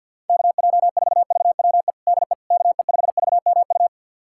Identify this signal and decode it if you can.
Morse code, heard as a keyed tone, about 48 wpm, 700 Hz. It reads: KJ4VWE DE KE5VMU
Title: USA Callsign DE Callsign - 6 Characters - 50wpm